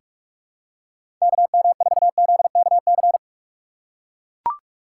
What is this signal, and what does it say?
Morse code, heard as a keyed tone, about 44 wpm, 700 Hz. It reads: KM4ZKC
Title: USA Call Signs - 6 Characters - 6x Character Spacing - 45wpm